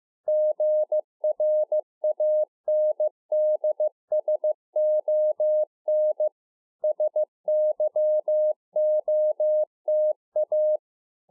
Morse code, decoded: GRANDSON SYOTA

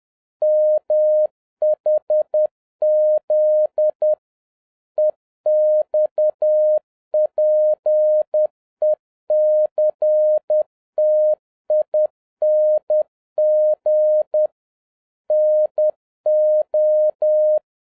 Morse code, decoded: MHZ EXPECTING NO

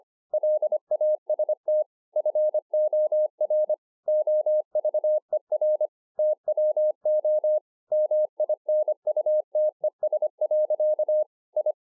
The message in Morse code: LAST FOR OVER TWO MINUTES. I

